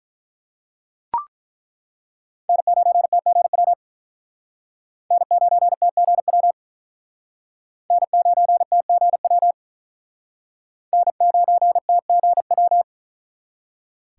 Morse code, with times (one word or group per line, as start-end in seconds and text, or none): 2.5-3.7 N9TGW
5.1-6.5 N9TGW
7.9-9.5 N9TGW
10.9-12.8 N9TGW